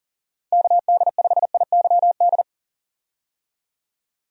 Morse code, heard as a keyed tone, about 40 wpm, 700 Hz. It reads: KD5IYD
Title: USA Call Signs - 6 Characters - 40wpm